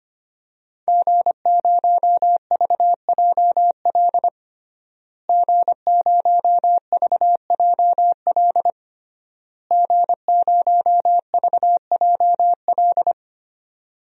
Morse code, heard as a keyed tone, about 25 wpm, 700 Hz. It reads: G0VJL G0VJL G0VJL
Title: DXCC-100 Call Signs - 5 Characters - Repeated 3x - 25wpm